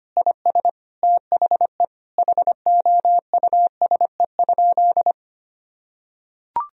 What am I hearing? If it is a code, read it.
Morse code, heard as a keyed tone, about 25 wpm, 700 Hz. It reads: IS THE HOUSE?
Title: Sentences from Top 1000 Words - Review Part 2 - 25wpm